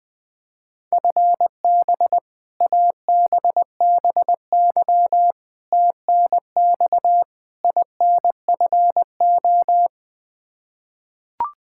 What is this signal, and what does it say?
Morse code, heard as a keyed tone, about 20 wpm, 700 Hz. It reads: FB ABBY TNX INFO
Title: Practice QSO - Exchange 3 - More Info - 20wpm